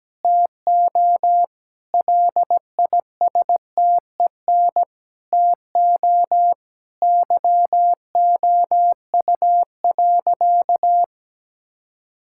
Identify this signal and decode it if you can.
Morse code, heard as a keyed tone, about 17 wpm, 700 Hz. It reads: TO LISTEN TO YOU.
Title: Sentences from Top 1600 Words 17wpm